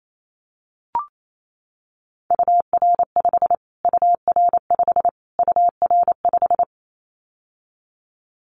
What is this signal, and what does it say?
Morse code, heard as a keyed tone, about 28 wpm, 700 Hz. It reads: UR5 UR5 UR5
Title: Callsign Prefixes for Top 25 DXCC - Repeated 3x - 28wpm